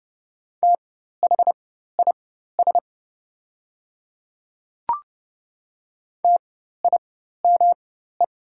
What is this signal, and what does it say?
Morse code, heard as a keyed tone, about 30 wpm, spaced as for 14 wpm, 700 Hz. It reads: THIS TIME
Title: 4x Character Spacing - Top 100 Words - 30wpm